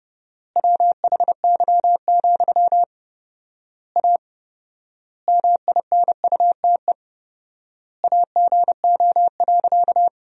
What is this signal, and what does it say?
Morse code, heard as a keyed tone, about 30 wpm, 700 Hz. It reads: WHY, A MINUTE AGO.
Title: 4x Word-Spacing - Sentences from Top 500 Words - 30wpm